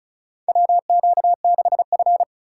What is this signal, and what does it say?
Morse code, heard as a keyed tone, about 36 wpm, 700 Hz. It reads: WQ6F
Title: USA Call Signs - 2x Character Spacing - 35wpm